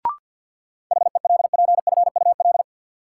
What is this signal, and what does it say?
Morse code, heard as a keyed tone, about 53 wpm, 700 Hz. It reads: HELPFUL
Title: Sentences from Top 1000 Words - Review Part 6 - New Word Drill - 50wpm